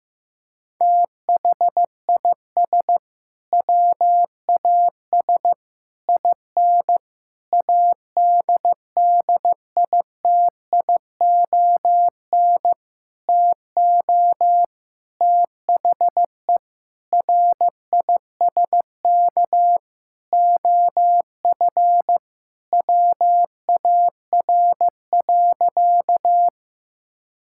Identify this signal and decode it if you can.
Morse code, heard as a keyed tone, about 15 wpm, 700 Hz. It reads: THIS WAS IN ADDITION TO THE RISK OF WAR.
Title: Sentences from Top 1000 Words - Review Part 6 - 15wpm